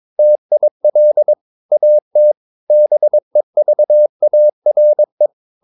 Morse code, heard as a keyed tone, about 22 wpm, 600 Hz. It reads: TIL AT BEVARE
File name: Ferie QRS 22wpm